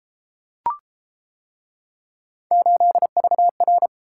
Morse code, heard as a keyed tone, about 31 wpm, 700 Hz. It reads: 8VR